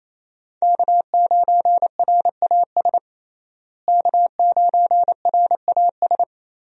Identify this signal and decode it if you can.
Morse code, heard as a keyed tone, about 28 wpm, 700 Hz. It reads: K9RAS K9RAS
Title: USA Call Signs - 5 Characters - Repeated 2x - 28wpm